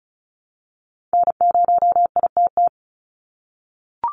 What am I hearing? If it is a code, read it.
Morse code, heard as a keyed tone, about 34 wpm, 700 Hz. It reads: N0ITT
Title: USA Call Signs 35wpm